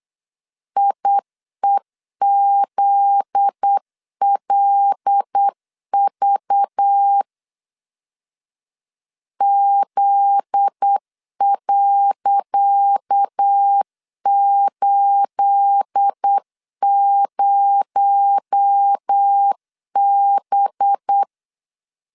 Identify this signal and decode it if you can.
Morse code, heard as a keyed tone, about 8 wpm, 800 Hz. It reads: IEZLV Z.80B
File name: LEZIONI / 10 SIGNS / 10C DOT 8wpm 3649